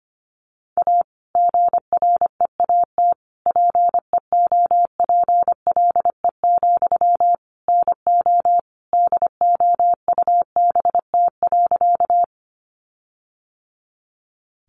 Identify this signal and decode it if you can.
Morse code, heard as a keyed tone, about 25 wpm, 700 Hz. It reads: A GREAT PEOPLE, NO DOUBT.
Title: Sentences from Top 1500 Words 25wpm